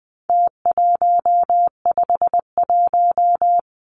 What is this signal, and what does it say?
Morse code, heard as a keyed tone, about 20 wpm, 700 Hz. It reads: T151
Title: Numbers - Contest Style - 4-Digit 20wpm